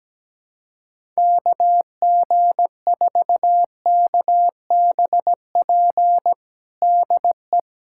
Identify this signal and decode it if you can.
Morse code, heard as a keyed tone, about 17 wpm, 700 Hz. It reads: KG4KBP DE
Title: USA Callsign DE Callsign 17wpm